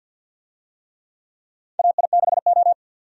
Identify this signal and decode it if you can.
Morse code, heard as a keyed tone, about 53 wpm, 700 Hz. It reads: AI6O
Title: USA Call Signs - 4 Characters - 50wpm